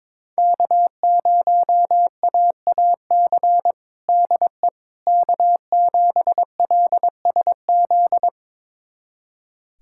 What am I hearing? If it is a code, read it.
Morse code, heard as a keyed tone, about 22 wpm, 700 Hz. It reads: K0AAC DE K7LSZ
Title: USA Callsign DE Callsign - 5 Characters - 22wpm